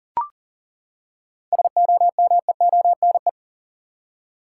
Morse code, heard as a keyed tone, about 38 wpm, 700 Hz. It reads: SOMEONE
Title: Seven Letter Words - 40wpm